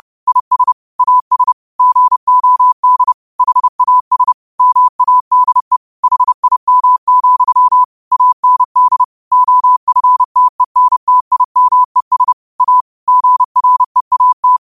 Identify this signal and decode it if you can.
Morse code, heard as a keyed tone, about 30 wpm, 1000 Hz. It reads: IS AS GOD HAS MADE HIM, AND OFTENTIMES A GREAT
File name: qotdfiles / 30wpm